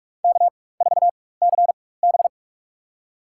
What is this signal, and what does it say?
Morse code, heard as a keyed tone, about 46 wpm, spaced as for 21 wpm, 700 Hz. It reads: K4CB
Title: USA Call Signs - 4 Characters - 4x Character Spacing - 45wpm